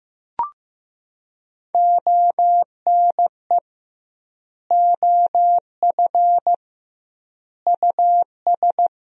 Morse code, heard as a keyed tone, about 15 wpm, 700 Hz. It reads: ONE OF US
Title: 2x Word-Spacing - Sentences from Top 300 Words - 15wpm